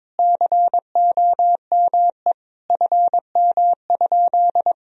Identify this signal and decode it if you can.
Morse code, heard as a keyed tone, about 21 wpm, 700 Hz. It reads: COME FM?
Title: Sentences from Top 200 Words and Abbreviations 22wpm